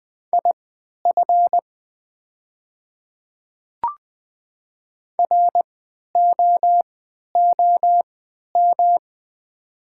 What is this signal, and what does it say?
Morse code, heard as a keyed tone, about 20 wpm, spaced as for 11 wpm, 700 Hz. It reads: IF ROOM
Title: 3x Character Spacing - Top 300 Words - 20wpm